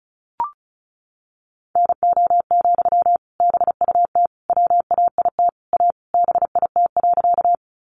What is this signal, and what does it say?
Morse code, heard as a keyed tone, about 34 wpm, 700 Hz. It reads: NO, BUT WAIT A BIT.
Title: Sentences from Top 900 Words 35wpm